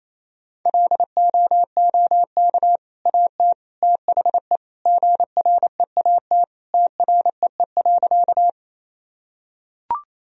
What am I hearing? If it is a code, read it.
Morse code, heard as a keyed tone, about 28 wpm, 700 Hz. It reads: LOOK AT THE GREAT TREE.
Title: Sentences from Top 1000 Words - Review Part 1 - 28wpm